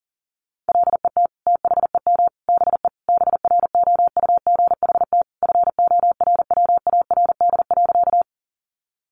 Morse code, heard as a keyed tone, about 40 wpm, 700 Hz. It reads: LET THEM BE BROUGHT FORWARD.
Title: Sentences from Top 1300 Words 40wpm